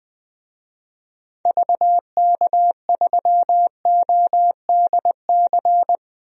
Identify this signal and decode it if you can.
Morse code, heard as a keyed tone, about 20 wpm, 700 Hz. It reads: VK3ODC
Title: DXCC-100 Call Signs - 6 Characters - 20wpm